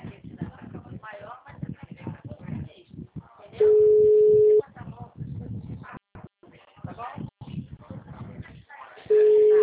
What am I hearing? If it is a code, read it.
Ringback tone (Europe, 425 Hz), with speech.
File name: out- -2034- - - .96653